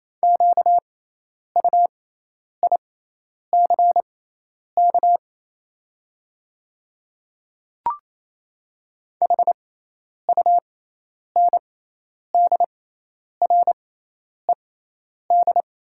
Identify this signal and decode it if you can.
Morse code, heard as a keyed tone, about 27 wpm, spaced as for 10 wpm, 700 Hz. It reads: QUICK HUNDRED
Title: 6x Character Spacing - Top 500 Words - 28wpm